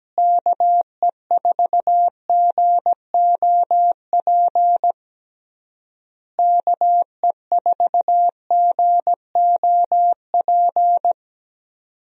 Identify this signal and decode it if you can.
Morse code, heard as a keyed tone, about 17 wpm, 700 Hz. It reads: KE4GOP KE4GOP
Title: USA Call Signs - 6 Characters - Repeated 3x - 17wpm